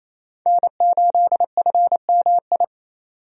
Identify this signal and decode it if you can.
Morse code, heard as a keyed tone, about 28 wpm, 700 Hz. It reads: N8FMI
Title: USA Call Signs - 5 Characters - 28wpm